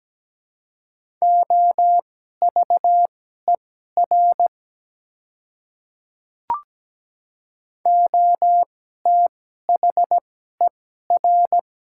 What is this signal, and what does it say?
Morse code, heard as a keyed tone, about 17 wpm, spaced as for 12 wpm, 700 Hz. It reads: OVER OTHER
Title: 2x Character Spacing - Top 200 Words - 17wpm